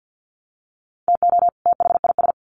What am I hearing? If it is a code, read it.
Morse code, heard as a keyed tone, about 50 wpm, 700 Hz. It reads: TO THIS
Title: Sets of 2 Words 50wpm